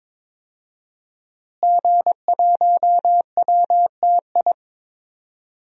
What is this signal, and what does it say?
Morse code, heard as a keyed tone, about 22 wpm, 700 Hz. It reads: G1WTI
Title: DXCC-100 Call Signs - 5 Characters - Repeated 2x - 22wpm